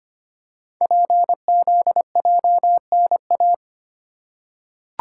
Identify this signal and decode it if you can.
Morse code, heard as a keyed tone, about 25 wpm, 700 Hz. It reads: PZJNA
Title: Code Groups - 5 Letters - 25wpm